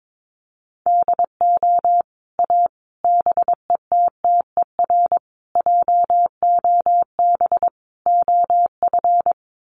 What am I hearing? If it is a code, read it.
Morse code, heard as a keyed tone, about 22 wpm, 700 Hz. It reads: DO A BETTER JOB OF
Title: Sets of 5 Words Encore2 22wpm